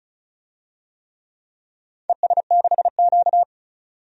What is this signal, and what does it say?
Morse code, heard as a keyed tone, about 36 wpm, 690 Hz. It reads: ES6Q
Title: DXCC-100 Call Signs - 4 Characters - 35wpm